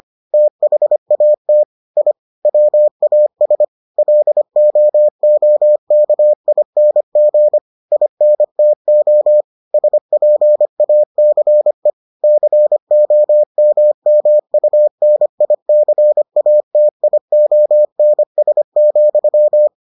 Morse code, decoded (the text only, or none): THAT I WAS LOOKING INTO SPACE COMMUNICATIONS,